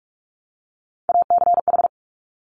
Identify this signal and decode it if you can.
Morse code, heard as a keyed tone, about 46 wpm, 700 Hz. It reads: ACH
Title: Top Three Letter Combinations in Words 45wpm